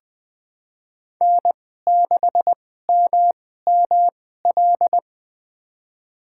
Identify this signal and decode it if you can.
Morse code, heard as a keyed tone, about 21 wpm, spaced as for 15 wpm, 700 Hz. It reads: N6MML
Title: USA Call Signs - 5 Characters - 2x Character Spacing - 20wpm